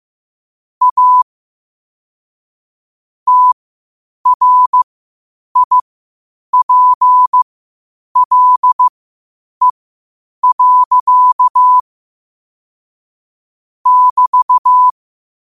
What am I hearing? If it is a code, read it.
Morse code, heard as a keyed tone, about 15 wpm, spaced as for 8 wpm, 1000 Hz. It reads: A TRIPLE. <BT>
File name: 8wpm